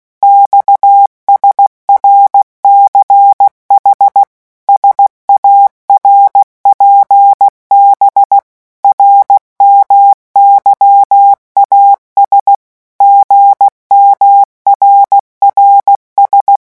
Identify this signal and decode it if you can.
Morse code, heard as a keyed tone, about 16 wpm, 800 Hz. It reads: XSRCH SARPB RMYAS GMRRS
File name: morse mp3 / revision5-vitesse 15 mots